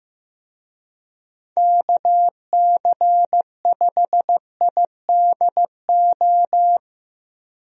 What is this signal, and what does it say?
Morse code, heard as a keyed tone, about 16 wpm, 700 Hz. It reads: KC5IDO
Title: USA Call Signs - 6 Characters - 15wpm